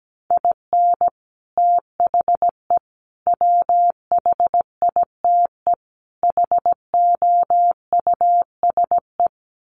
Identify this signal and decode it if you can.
Morse code, heard as a keyed tone, about 17 wpm, 700 Hz. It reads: IN THE WHITE HOUSE
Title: Sets of 4 Words 17wpm